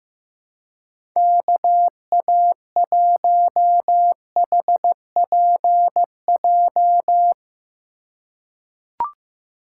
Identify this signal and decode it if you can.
Morse code, heard as a keyed tone, about 16 wpm, 700 Hz. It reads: KA1HPJ
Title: USA Call Signs - 6 Characters - 3x Character Spacing - 15wpm